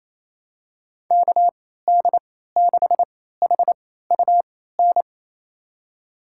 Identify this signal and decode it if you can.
Morse code, heard as a keyed tone, about 28 wpm, spaced as for 16 wpm, 700 Hz. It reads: KD6HUN
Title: USA Call Signs - 6 Characters - 3x Character Spacing - 28wpm